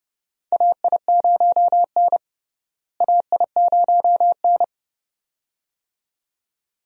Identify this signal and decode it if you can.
Morse code, heard as a keyed tone, about 30 wpm, 700 Hz. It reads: AI0N AI0N
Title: USA Call Signs - 4 Characters - Repeated 2x - 30wpm